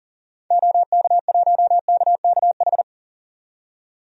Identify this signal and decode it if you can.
Morse code, heard as a keyed tone, about 40 wpm, 700 Hz. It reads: OK1KKH
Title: DXCC-100 Call Signs - 6 Characters - 40wpm